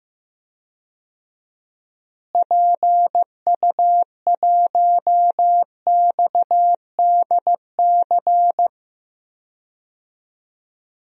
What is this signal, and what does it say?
Morse code, heard as a keyed tone, about 15 wpm, 700 Hz. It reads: PU1XDC
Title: DXCC-100 Call Signs - 6 Characters - Repeated 2x - 15wpm